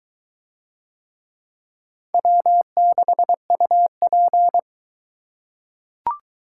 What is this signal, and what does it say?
Morse code, heard as a keyed tone, about 21 wpm, 700 Hz. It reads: W6UP